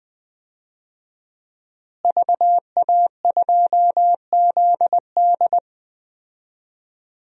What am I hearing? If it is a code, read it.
Morse code, heard as a keyed tone, about 21 wpm, 700 Hz. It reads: VA2ZD
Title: DXCC-100 Call Signs - 5 Characters - Repeated 3x - 20wpm